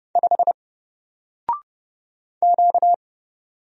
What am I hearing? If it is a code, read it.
Morse code, heard as a keyed tone, about 31 wpm, 700 Hz. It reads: 5 Q
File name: Single-Letter-Number-30wpm